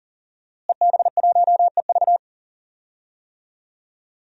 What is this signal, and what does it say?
Morse code, heard as a keyed tone, about 40 wpm, 700 Hz. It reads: EB1EV
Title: DXCC-100 Call Signs - 5 Characters - Repeated 2x - 40wpm